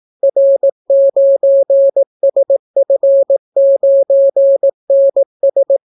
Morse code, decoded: R9SF9NS

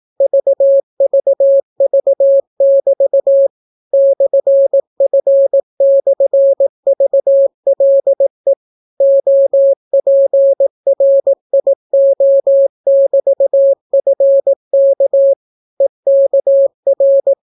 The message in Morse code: VVV<BT> /F/VLE OPRIO<BT>FK EKR

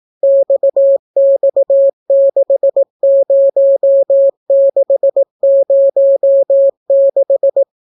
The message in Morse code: XX60606

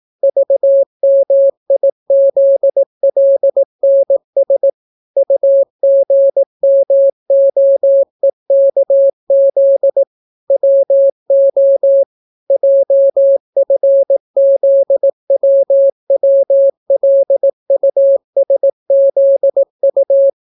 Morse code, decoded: VMIZLNS UGMOEKZ WO JFZWWLUSZU

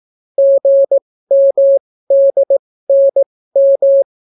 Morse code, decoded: GMDNM

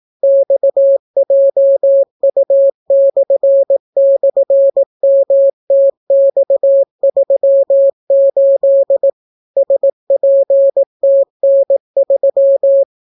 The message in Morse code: XJU//MTX38 SPTN3

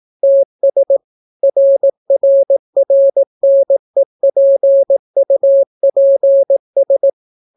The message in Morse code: TS RRRNEPUPS

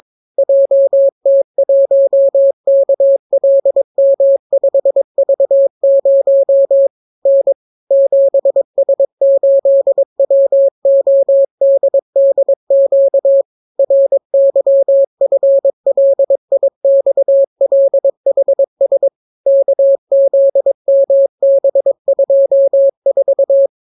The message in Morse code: JT1KLM5V0 N 7S8WODDQ RYFLIXLHS KZMB24

